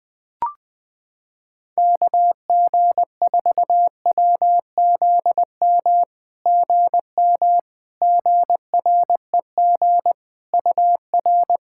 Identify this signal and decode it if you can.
Morse code, heard as a keyed tone, about 20 wpm, 700 Hz. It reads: KG4WZM GM GREG UR
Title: SOTA QSO Practice 20wpm